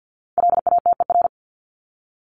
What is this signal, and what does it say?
Morse code, heard as a keyed tone, about 50 wpm, 700 Hz. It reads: LATER
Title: Top 501-750 Words 50wpm